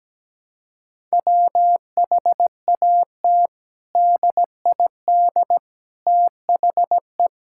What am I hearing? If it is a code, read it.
Morse code, heard as a keyed tone, about 17 wpm, 700 Hz. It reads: WHAT DID THE
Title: Sentences from Top 1000 Words - Review Part 6 - 17wpm